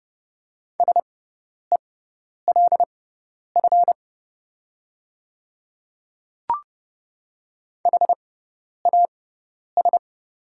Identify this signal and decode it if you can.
Morse code, heard as a keyed tone, about 30 wpm, spaced as for 10 wpm, 700 Hz. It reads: SELF HAS
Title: 6x Character Spacing - Top 300 Words - 30wpm